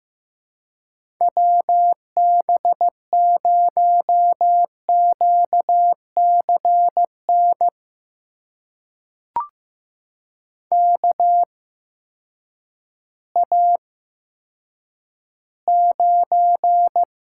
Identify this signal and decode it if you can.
Morse code, heard as a keyed tone, about 15 wpm, spaced as for 9 wpm, 700 Hz. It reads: WB0QCN KA9
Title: USA Call Signs - 6 Characters - 8x Character Spacing - 15wpm